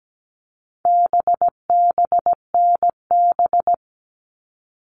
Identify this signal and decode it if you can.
Morse code, heard as a keyed tone, about 17 wpm, 700 Hz. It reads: BBNB
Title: All Cut Numbers - 4-Digit 17wpm